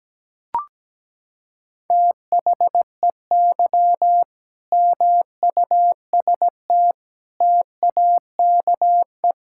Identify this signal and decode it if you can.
Morse code, heard as a keyed tone, about 17 wpm, 700 Hz. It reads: THEY MUST TAKE
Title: Sentences from Top 1000 Words - Review Part 6 - 17wpm